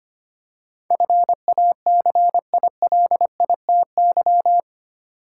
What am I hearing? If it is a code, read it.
Morse code, heard as a keyed tone, about 25 wpm, 700 Hz. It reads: FACILITY
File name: 8 Letter Words - First 3 Letters Given - 25wpm